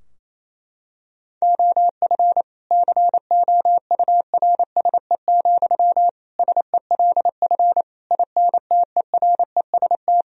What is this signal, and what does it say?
Morse code, heard as a keyed tone, about 28 wpm, 700 Hz. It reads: OF COURSE, SELF INTEREST
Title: Sentences from Top 1000 Words - Review Part 2 - 28wpm